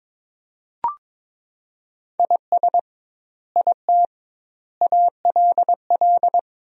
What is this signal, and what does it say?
Morse code, heard as a keyed tone, about 22 wpm, 700 Hz. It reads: IS IT ALL
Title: 2x Word-Spacing - Sentences from Top 200 Words - 22wpm